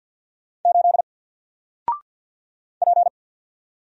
Morse code, heard as a keyed tone, about 54 wpm, 700 Hz. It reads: 8 P
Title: Single Character (Letter:Number:Punctuation:Prosign) Alt 50wpm